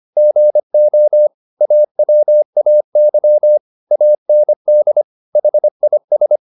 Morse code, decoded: GO AWAY AND HIS

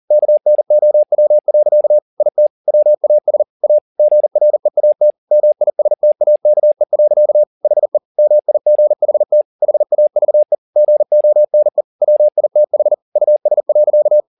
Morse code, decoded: KNOW. IT WAS A GREAT MISTAKE. HE MIGHT HAVE GONE WITH US.

